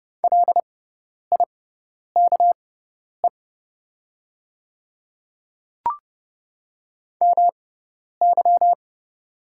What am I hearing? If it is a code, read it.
Morse code, heard as a keyed tone, about 30 wpm, spaced as for 10 wpm, 700 Hz. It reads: LIKE MY
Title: 6x Character Spacing - Top 100 Words - 30wpm